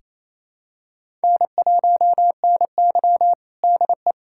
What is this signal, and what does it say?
Morse code, heard as a keyed tone, about 28 wpm, 700 Hz. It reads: N1NY DE
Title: USA Callsign DE Callsign - 4 Characters - 28wpm